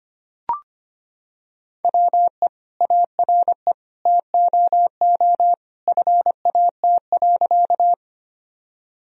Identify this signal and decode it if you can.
Morse code, heard as a keyed tone, about 25 wpm, 700 Hz. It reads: WE ARE TOO FAT.
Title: Sentences from Top 1000 Words - Review Part 4 - 25wpm